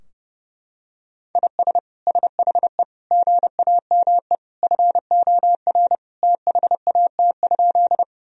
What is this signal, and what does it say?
Morse code, heard as a keyed tone, about 32 wpm, 700 Hz. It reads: IS SHE GAME FOR THAT?
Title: Sentences from Top 1000 Words - Review Part 3 - 30wpm